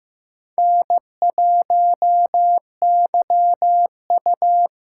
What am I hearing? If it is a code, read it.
Morse code, heard as a keyed tone, about 15 wpm, 700 Hz. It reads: N1YU
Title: USA Call Signs - 6x Character Spacing - 15wpm